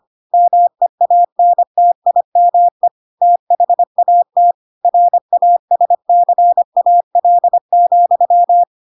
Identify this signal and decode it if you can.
Morse code, heard as a keyed tone, about 25 wpm, 700 Hz. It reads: MEANTIME THAT RASCAL,
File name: Pinocchio / Pinocchio ch4 25 25 ik1hge